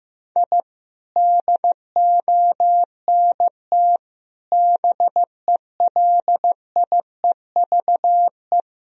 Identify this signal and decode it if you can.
Morse code, heard as a keyed tone, about 16 wpm, 700 Hz. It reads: I DONT BELIEVE
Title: Sentences from Top 1000 Words - Review Part 3 - 15wpm